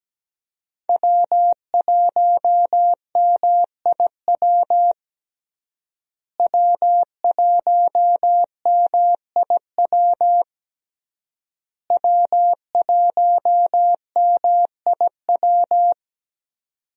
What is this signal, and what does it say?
Morse code, heard as a keyed tone, about 17 wpm, 700 Hz. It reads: W1MIW W1MIW W1MIW
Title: USA Call Signs - 5 Characters - Repeated 3x - 17wpm